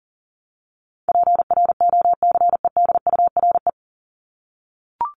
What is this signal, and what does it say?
Morse code, heard as a keyed tone, about 40 wpm, 700 Hz. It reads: PROCEDURE
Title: Top 1751-2000 Words 40wpm